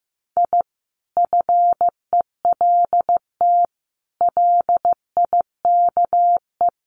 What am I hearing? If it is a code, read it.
Morse code, heard as a keyed tone, about 15 wpm, 700 Hz. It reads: I FELT LIKE
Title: Sets of 5 Words Encore 15wpm